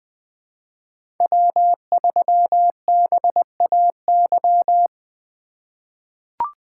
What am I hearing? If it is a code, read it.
Morse code, heard as a keyed tone, about 20 wpm, 700 Hz. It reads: W3BAY
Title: 2023 POTA Activator Call Signs - 20wpm